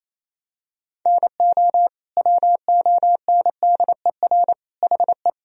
Morse code, heard as a keyed tone, about 28 wpm, 700 Hz. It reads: NO WONDER HE
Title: Sentences from Top 1000 Words - Review Part 3 - 28wpm